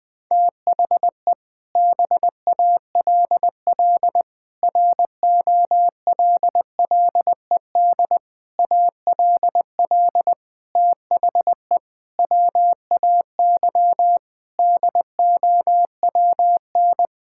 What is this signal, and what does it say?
Morse code, heard as a keyed tone, about 20 wpm, 700 Hz. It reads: THE BALL ROLLED ALL THE WAY DOWN
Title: Sentences with 6-Word Phrases - 20wpm